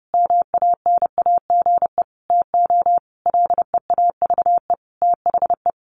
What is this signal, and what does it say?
Morse code, heard as a keyed tone, about 30 wpm, 700 Hz. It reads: MANAGE TO LEAVE THE
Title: Sentences from Top 1400 Words 30wpm